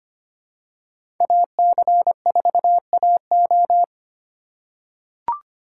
Morse code, heard as a keyed tone, about 25 wpm, 700 Hz. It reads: AC4AO
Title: USA Call Signs - 5 Characters - 2x Character Spacing - 25wpm